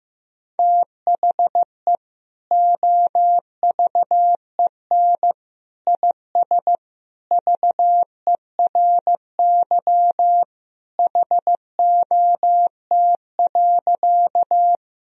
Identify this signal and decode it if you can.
Morse code, heard as a keyed tone, about 15 wpm, 700 Hz. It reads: THE OVEN IS VERY HOT.
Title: Sentences from Top 1000 Words - Review Part 4 - 15wpm